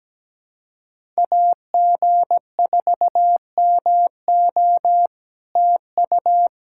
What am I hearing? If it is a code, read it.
Morse code, heard as a keyed tone, about 17 wpm, 700 Hz. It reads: AG4MO TU
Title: SOTA QSO Practice 17wpm